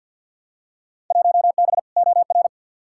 Morse code, heard as a keyed tone, about 54 wpm, 700 Hz. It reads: 1B OR
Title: Field Day Exchanges - 50wpm